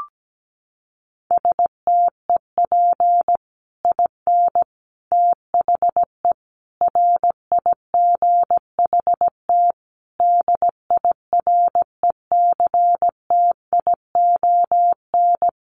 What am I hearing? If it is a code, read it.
Morse code, heard as a keyed tone, about 17 wpm, 700 Hz. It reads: STEP IN THE RIGHT DIRECTION
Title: Sets of 5 Words Encore 17wpm